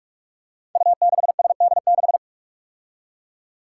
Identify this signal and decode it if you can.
Morse code, heard as a keyed tone, about 44 wpm, 700 Hz. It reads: U6SD6
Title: Five Challenging Letter-Number Combinations 45wpm